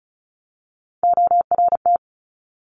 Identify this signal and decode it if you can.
Morse code, heard as a keyed tone, about 36 wpm, 700 Hz. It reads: ORT
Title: Top Three Letter Combinations in Words 35wpm